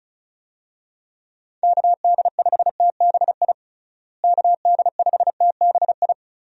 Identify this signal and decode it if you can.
Morse code, heard as a keyed tone, about 36 wpm, 700 Hz. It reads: KD5TBI KD5TBI
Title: USA Call Signs - 6 Characters - Repeated 2x - 35wpm